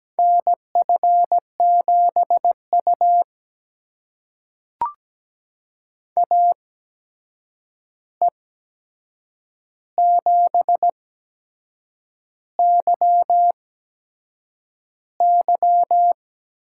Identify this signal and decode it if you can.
Morse code, heard as a keyed tone, about 17 wpm, spaced as for 7 wpm, 700 Hz. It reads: NF7U AE7YY
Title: USA Call Signs - 8x Character Spacing - 17wpm